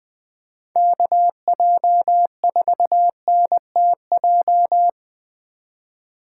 Morse code, heard as a keyed tone, about 20 wpm, 700 Hz. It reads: KJ4NTJ
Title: USA Call Signs - 6 Characters - 2x Character Spacing - 20wpm